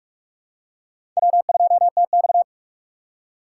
Morse code, heard as a keyed tone, about 44 wpm, 700 Hz. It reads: W2TX
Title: USA Call Signs - 4 Characters - 2x Character Spacing - 45wpm